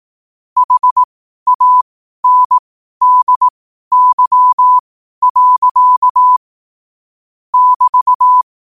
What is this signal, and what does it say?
Morse code, heard as a keyed tone, about 18 wpm, spaced as for 12 wpm, 1000 Hz. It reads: HANDY. <BT>
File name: qotdfiles / 12wpm